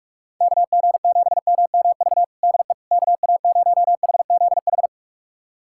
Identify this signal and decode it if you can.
Morse code, heard as a keyed tone, about 46 wpm, 700 Hz. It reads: KG7MMV DE KA0HZH
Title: USA Callsign DE Callsign 45wpm